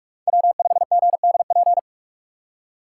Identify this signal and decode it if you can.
Morse code, heard as a keyed tone, about 58 wpm, 700 Hz. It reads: W5GDP